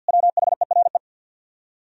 Morse code, heard as a keyed tone, about 64 wpm, 700 Hz. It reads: WHERE